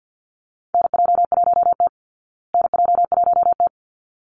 Morse code, heard as a keyed tone, about 50 wpm, 700 Hz. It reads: NJ1T NJ1T
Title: Call Signs Repeated 3x - 50wpm